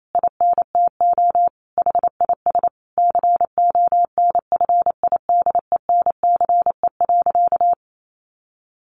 Morse code, heard as a keyed tone, about 28 wpm, 700 Hz. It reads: INTO HIS CONFIDENCE.
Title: Sentences from Top 1500 Words 28wpm